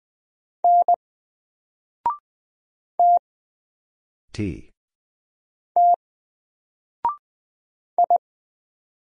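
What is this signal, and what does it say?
Morse code, heard as a keyed tone, about 20 wpm, 700 Hz. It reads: N T T I
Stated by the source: Single letter and number Morse drill at 20 wpm, from N4LCD.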